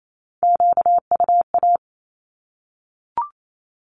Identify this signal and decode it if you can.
Morse code, heard as a keyed tone, about 28 wpm, 700 Hz. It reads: QUA
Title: Q-Codes 28wpm